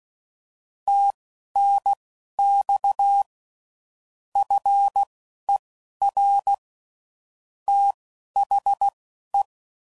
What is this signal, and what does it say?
Morse code, heard as a keyed tone, about 16 wpm, spaced as for 11 wpm, 800 Hz. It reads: TNX FER THE
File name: CW Code Groups @10WPM / QSOs / QSO kc7haa